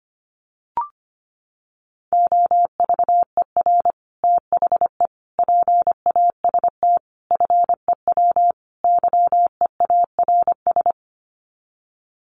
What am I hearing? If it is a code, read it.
Morse code, heard as a keyed tone, about 25 wpm, 700 Hz. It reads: OVER THE PAST FEW YEARS
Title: Sets of 5 Words Encore 25wpm